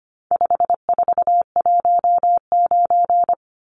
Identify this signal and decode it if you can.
Morse code, heard as a keyed tone, about 25 wpm, 700 Hz. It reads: 5419
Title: Numbers - 4-Digit 25wpm